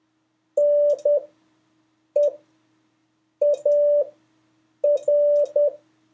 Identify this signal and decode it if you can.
Morse code, heard as a keyed tone, about 10 wpm, spaced as for 6 wpm, 580 Hz. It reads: NEAR
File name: 2of5-words-1